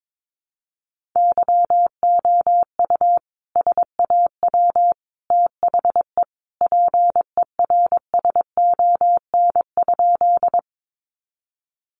Morse code, heard as a keyed tone, about 22 wpm, 700 Hz. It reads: YOU SAW THE PERSON?
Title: Sentences from Top 400 Words 22wpm